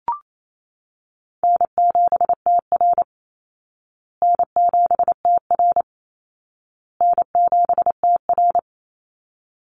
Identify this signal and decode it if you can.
Morse code, heard as a keyed tone, about 28 wpm, 700 Hz. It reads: N7TR N7TR N7TR
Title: Call Signs Repeated 2x - 28wpm